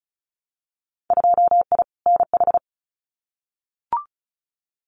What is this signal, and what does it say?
Morse code, heard as a keyed tone, about 36 wpm, 700 Hz. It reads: 2I NH
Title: Winter Field Day Exchanges Encore 35wpm